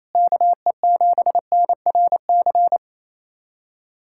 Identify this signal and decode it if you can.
Morse code, heard as a keyed tone, about 28 wpm, 700 Hz. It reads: KE7NRC
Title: USA Callsign DE Callsign - 6 Characters - 28wpm